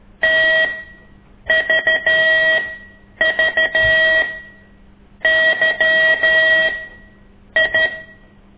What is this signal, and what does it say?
Morse code, heard as a keyed tone, about 13 wpm, spaced as for 7 wpm, 1900 Hz. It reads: TVVYI
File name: ik1weg2 / CORSI CW